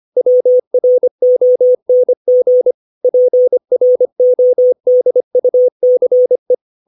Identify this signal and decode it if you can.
Morse code, heard as a keyed tone, about 25 wpm, 500 Hz. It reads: WRONG PRODUCE
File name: 632 words / 632 Words 25WPM